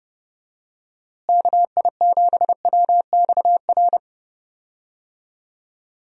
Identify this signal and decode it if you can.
Morse code, heard as a keyed tone, about 30 wpm, 700 Hz. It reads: KI7WXR
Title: USA Call Signs - 6 Characters - Repeated 3x - 30wpm